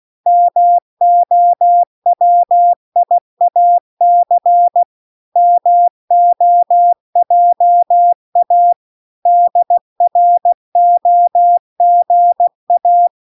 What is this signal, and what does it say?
Morse code, heard as a keyed tone, about 16 wpm, 700 Hz. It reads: MOWIAC MOJA DROGA